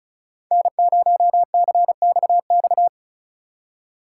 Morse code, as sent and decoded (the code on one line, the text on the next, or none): -. ----- -.-. -..- -..-
N0CXX